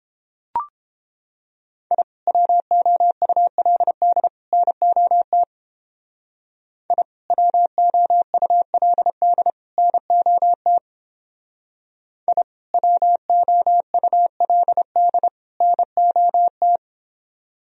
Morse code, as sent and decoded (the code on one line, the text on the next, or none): .. / .-- --- ..- .-.. -.. / -. --- - / .. / .-- --- ..- .-.. -.. / -. --- - / .. / .-- --- ..- .-.. -.. / -. --- -
I WOULD NOT I WOULD NOT I WOULD NOT